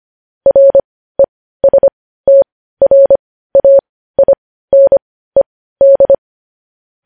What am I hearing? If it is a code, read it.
Morse code, heard as a keyed tone, about 25 wpm, spaced as for 15 wpm, 550 Hz. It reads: RESTRAINED
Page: news-headlines-in-morse-code-at-15-wpm